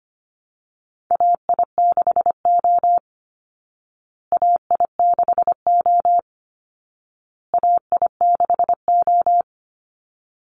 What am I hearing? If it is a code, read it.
Morse code, heard as a keyed tone, about 25 wpm, 700 Hz. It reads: AI6O AI6O AI6O
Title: Call Signs Repeated 2x - 25wpm